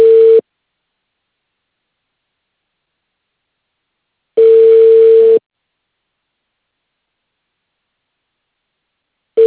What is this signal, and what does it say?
Ringback tone (China, 450 Hz).